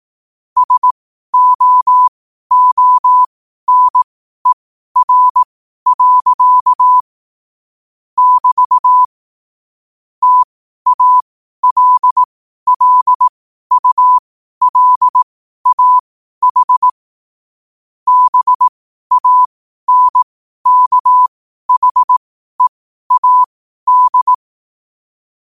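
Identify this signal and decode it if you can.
Morse code, heard as a keyed tone, about 18 wpm, spaced as for 12 wpm, 1000 Hz. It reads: SOONER. <BT> TALLULAH BANKHEAD
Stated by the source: Morse code at 12 words per minute.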